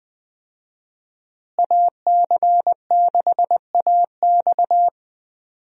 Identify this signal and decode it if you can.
Morse code, heard as a keyed tone, about 20 wpm, 700 Hz. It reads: AC6AX
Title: USA Call Signs - 5 Characters - Repeated 3x - 20wpm